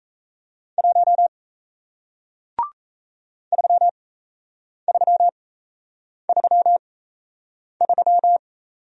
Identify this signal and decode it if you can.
Morse code, heard as a keyed tone, about 38 wpm, spaced as for 8 wpm, 700 Hz. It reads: 1 3333